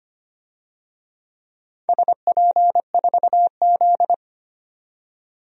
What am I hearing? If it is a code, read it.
Morse code, heard as a keyed tone, about 25 wpm, 700 Hz. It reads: SP4Z
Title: DXCC-100 Call Signs - 4 Characters - 25wpm